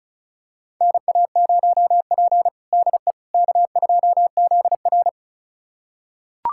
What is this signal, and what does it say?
Morse code, heard as a keyed tone, about 36 wpm, 700 Hz. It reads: NA0P DE K2ZR
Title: USA Callsign DE Callsign - 4 Characters - 35wpm